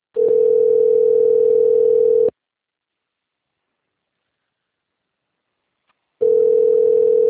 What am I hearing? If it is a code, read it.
Ringback tone (North America), 440 + 480 Hz.
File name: rg-1600-unknown- - - .1870